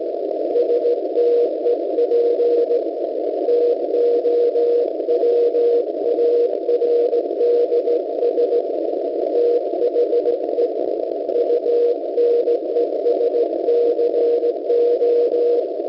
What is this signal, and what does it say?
Morse code, heard as a keyed tone, about 17 wpm, 480 Hz. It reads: STEP TOWARDS THE UNESCO